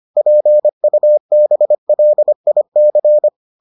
Morse code, heard as a keyed tone, about 25 wpm, 600 Hz. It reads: PUBLIC